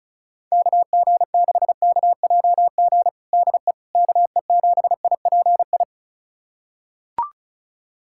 Morse code, heard as a keyed tone, about 36 wpm, 700 Hz. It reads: KG6KJG DE KE7IPI
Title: USA Callsign DE Callsign - 6 Characters - 35wpm